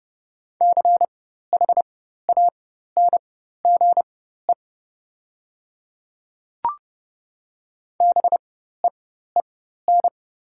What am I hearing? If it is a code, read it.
Morse code, heard as a keyed tone, about 30 wpm, spaced as for 14 wpm, 700 Hz. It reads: CHANGE BEEN
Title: 4x Character Spacing - Top 500 Words - 30wpm